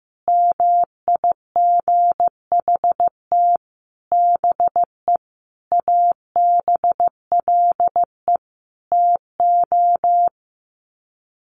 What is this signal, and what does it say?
Morse code, heard as a keyed tone, about 15 wpm, 700 Hz. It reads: MIGHT BE ABLE TO
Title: Sets of 5 Words Encore2 15wpm